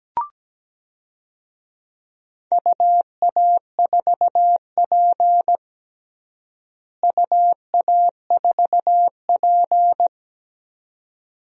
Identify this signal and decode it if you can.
Morse code, heard as a keyed tone, about 17 wpm, 700 Hz. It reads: UA4P UA4P
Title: DXCC-100 Call Signs - 4 Characters - Repeated 2x - 17wpm